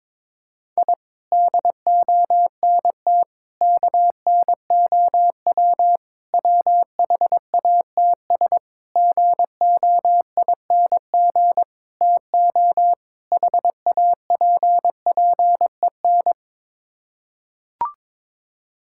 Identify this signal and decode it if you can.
Morse code, heard as a keyed tone, about 22 wpm, 700 Hz. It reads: I DONT KNOW WHATS GOING TO HAPPEN
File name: practice / 22wpm / Sets of 7 Words - 22wpm